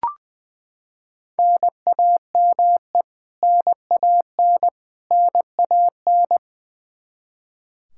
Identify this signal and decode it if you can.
Morse code, heard as a keyed tone, about 20 wpm, 700 Hz. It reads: NAME NAN NAN
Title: Practice QSO - Exchange 1 - Beginning - 20wpm